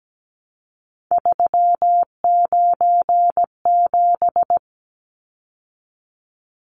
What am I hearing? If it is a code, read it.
Morse code, heard as a keyed tone, about 17 wpm, 700 Hz. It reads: 397
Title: Numbers - 3-Digit 17wpm